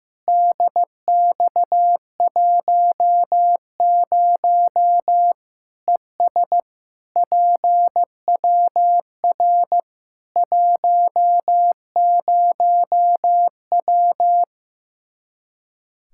Morse code, heard as a keyed tone, about 15 wpm, 700 Hz. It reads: DX10 ES PWR 10W
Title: Practice QSO - Exchange 2 - Info - 15wpm